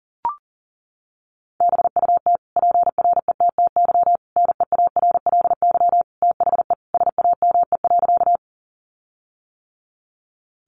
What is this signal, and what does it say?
Morse code, heard as a keyed tone, about 40 wpm, 700 Hz. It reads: BUT PRETTY NEARLY THE SAME.
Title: Sentences from Top 1300 Words 40wpm